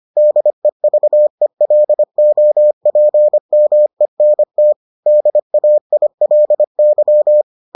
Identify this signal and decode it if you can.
Morse code, heard as a keyed tone, about 25 wpm, 600 Hz. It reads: DEVELOPMENT DAILY